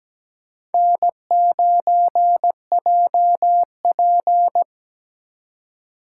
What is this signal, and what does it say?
Morse code, heard as a keyed tone, about 20 wpm, 700 Hz. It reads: N9JP
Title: Speed Racing - USA Call Signs 17wpm